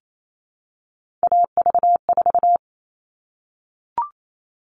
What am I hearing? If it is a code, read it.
Morse code, heard as a keyed tone, about 28 wpm, 700 Hz. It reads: AV4
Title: All Cut Numbers - 3-Digit 28wpm